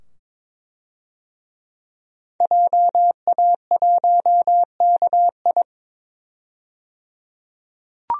Morse code, heard as a keyed tone, about 22 wpm, 700 Hz. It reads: JA1KI
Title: DXCC-100 Call Signs - 5 Characters - 22wpm